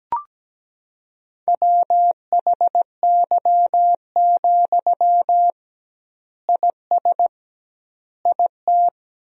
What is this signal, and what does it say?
Morse code, heard as a keyed tone, about 17 wpm, 700 Hz. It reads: WHY, IS IT
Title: 2x Word-Spacing - Sentences from Top 300 Words - 17wpm